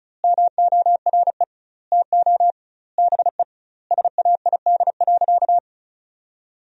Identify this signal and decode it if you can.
Morse code, heard as a keyed tone, about 34 wpm, 700 Hz. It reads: MORE TO BE SAID.
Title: 2x Word-Spacing - Sentences from Top 500 Words - 35wpm